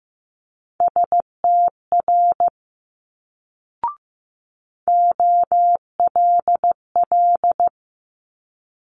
Morse code, heard as a keyed tone, about 15 wpm, 700 Hz. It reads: STR OLL
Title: Multi-letter Phonemes Quick Fire 15wpm